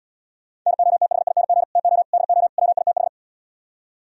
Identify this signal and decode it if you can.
Morse code, heard as a keyed tone, about 77 wpm, 700 Hz. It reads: NOTHING TO DO WITH